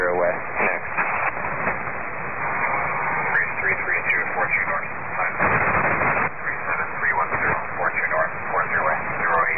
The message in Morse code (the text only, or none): none